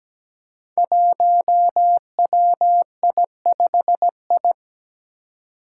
Morse code, heard as a keyed tone, about 17 wpm, 700 Hz. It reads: 1WI5I
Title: Five Challenging Letter-Number Combinations 17wpm